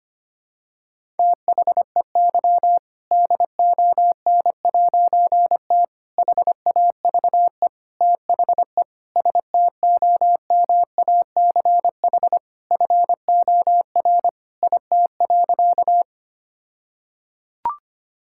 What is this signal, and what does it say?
Morse code, heard as a keyed tone, about 25 wpm, 700 Hz. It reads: THEY DON'T HAVE THE STOMACH FOR IT.
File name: practice / 25wpm / All-Age Appropriate Jokes - 25wpm